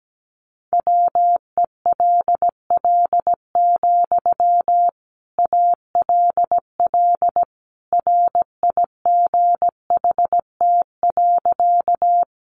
Morse code, heard as a keyed tone, about 17 wpm, 700 Hz. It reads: WELL, ALL RIGHT.
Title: Sentences from Top 200 Words 17wpm